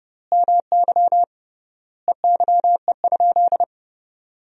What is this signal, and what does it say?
Morse code, heard as a keyed tone, about 30 wpm, 700 Hz. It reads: MY EYE?
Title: 3x Word-Spacing - Sentences from Top 500 Words - 30wpm